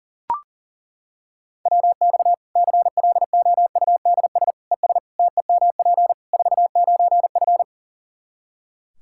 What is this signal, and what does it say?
Morse code, heard as a keyed tone, about 40 wpm, 700 Hz. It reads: WX CLOUDS ES TEMP 49F
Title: K1USN SST QSO Practice 40wpm